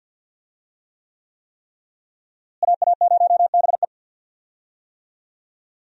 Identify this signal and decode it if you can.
Morse code, heard as a keyed tone, about 49 wpm, 700 Hz. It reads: AA0BE
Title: USA Call Signs - 5 Characters - Repeated 3x - 50wpm